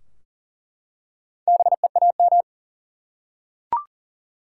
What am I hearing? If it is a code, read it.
Morse code, heard as a keyed tone, about 40 wpm, 700 Hz. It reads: BEAM
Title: Top 90 QSO Elements 40wpm